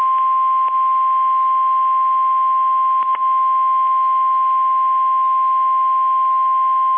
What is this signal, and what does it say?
Emergency Alert System (SAME) tones (the 1050 Hz attention tone). It is heard from 0.3 to 6.2 s.
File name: eas / KSDZKDJL / KSDZKDJL